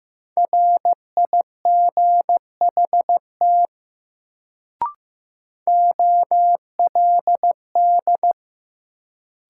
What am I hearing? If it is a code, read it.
Morse code, heard as a keyed tone, about 15 wpm, 700 Hz. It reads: RIGHT OLD
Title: Sentences from Top 200 Words - New Word Drill - 15wpm